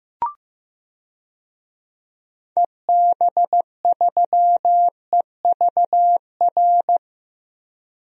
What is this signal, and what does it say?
Morse code, heard as a keyed tone, about 15 wpm, 700 Hz. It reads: EB3EVR
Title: DXCC-100 Call Signs - 6 Characters - Repeated 3x - 15wpm